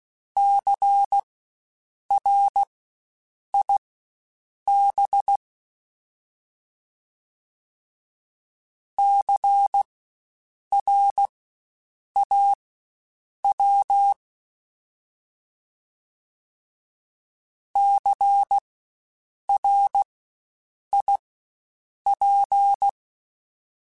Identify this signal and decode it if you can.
Morse code, heard as a keyed tone, about 16 wpm, spaced as for 7 wpm, 800 Hz. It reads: CRIB CRAW CRIP
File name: Spaced Alphas / Spaced Alphas mp3s / Cs sp @15